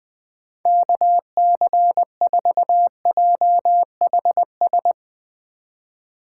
Morse code, heard as a keyed tone, about 20 wpm, 700 Hz. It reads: KC4JHS
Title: USA Call Signs - 4x Character Spacing - 20wpm